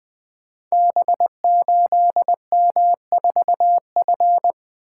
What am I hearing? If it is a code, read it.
Morse code, heard as a keyed tone, about 20 wpm, 700 Hz. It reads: B8M4F
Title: Code Groups - 5 Letter:Numbers - 20wpm